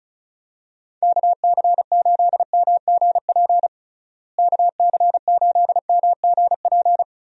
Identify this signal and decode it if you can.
Morse code, heard as a keyed tone, about 34 wpm, 700 Hz. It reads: KC8MGP KC8MGP
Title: USA Call Signs - 6 Characters - Repeated 2x - 35wpm